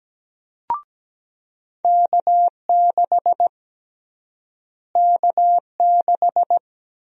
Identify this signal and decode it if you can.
Morse code, heard as a keyed tone, about 17 wpm, 700 Hz. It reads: K6 K6
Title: USA Call Sign Prefixes - Repeated 2x - 17wpm